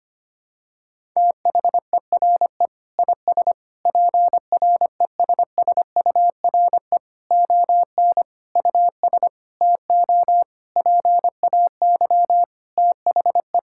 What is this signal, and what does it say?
Morse code, heard as a keyed tone, about 25 wpm, 700 Hz. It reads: THERE IS PRESSURE ON US TO PAY THE
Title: Sentences from Top 1000 Words - Review Part 6 - 25wpm